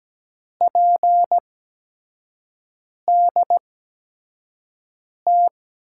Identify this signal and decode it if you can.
Morse code, heard as a keyed tone, about 17 wpm, spaced as for 5 wpm, 700 Hz. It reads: PDT
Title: USA Call Signs - 6 Characters - 8x Character Spacing - 17wpm